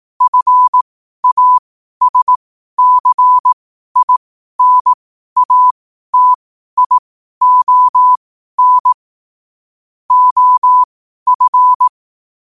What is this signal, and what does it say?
Morse code, heard as a keyed tone, about 18 wpm, spaced as for 12 wpm, 1000 Hz. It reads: FASCINATION OF
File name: qotdfiles / 12wpm